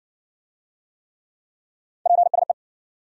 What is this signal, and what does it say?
Morse code, heard as a keyed tone, about 53 wpm, 700 Hz. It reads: PSE